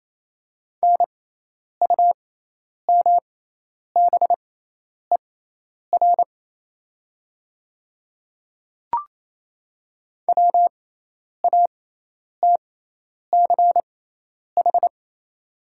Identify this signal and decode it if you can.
Morse code, heard as a keyed tone, about 28 wpm, spaced as for 10 wpm, 700 Hz. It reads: NUMBER WATCH
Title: 6x Character Spacing - Top 300 Words - 28wpm